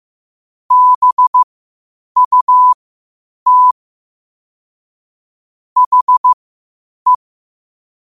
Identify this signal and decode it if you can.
Morse code, heard as a keyed tone, about 15 wpm, spaced as for 8 wpm, 1000 Hz. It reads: BUT HE
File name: qotdfiles / 8wpm